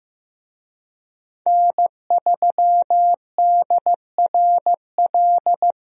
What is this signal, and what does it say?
Morse code, heard as a keyed tone, about 15 wpm, 700 Hz. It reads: N3DRL
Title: USA Call Signs - 5 Characters - Repeated 2x - 15wpm